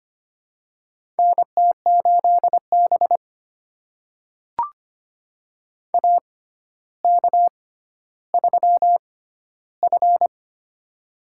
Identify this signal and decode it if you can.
Morse code, heard as a keyed tone, about 25 wpm, spaced as for 13 wpm, 700 Hz. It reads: NT8B AK3F
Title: USA Call Signs - 6x Character Spacing - 25wpm